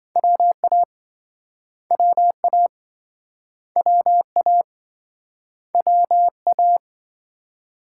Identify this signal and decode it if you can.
Morse code, heard as a keyed tone, about 26 wpm, 700 Hz. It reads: WA WA WA WA